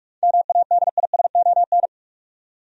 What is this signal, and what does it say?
Morse code, heard as a keyed tone, about 46 wpm, 700 Hz. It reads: MADISON
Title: USA Common Names - 45wpm